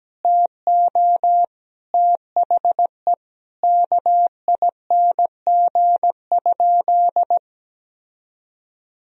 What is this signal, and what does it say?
Morse code, heard as a keyed tone, about 17 wpm, 700 Hz. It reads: TO THE KING?
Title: Sentences from Top 1000 Words - Review Part 4 - 17wpm